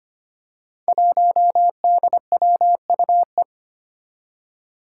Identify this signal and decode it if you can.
Morse code, heard as a keyed tone, about 25 wpm, 700 Hz. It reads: 1DWUE
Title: Five Challenging Letter-Number Combinations 25wpm